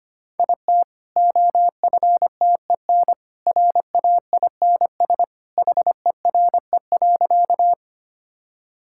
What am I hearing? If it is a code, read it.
Morse code, heard as a keyed tone, about 25 wpm, 700 Hz. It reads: IT OFTEN RAINS HERE.
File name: Easy Sentences with Hints - 25wpm